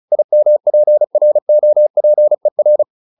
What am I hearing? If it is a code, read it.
Morse code, heard as a keyed tone, about 35 wpm, 600 Hz. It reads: IMPROPER